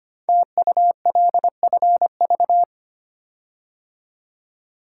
Morse code, heard as a keyed tone, about 25 wpm, 700 Hz. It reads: TULFV
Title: Code Groups - 5 Letters - 25wpm